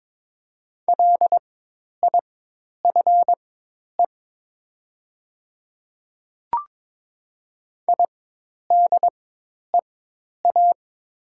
Morse code, heard as a keyed tone, about 22 wpm, spaced as for 10 wpm, 700 Hz. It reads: LIFE IDEA
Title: 4x Character Spacing - Top 300 Words - 22wpm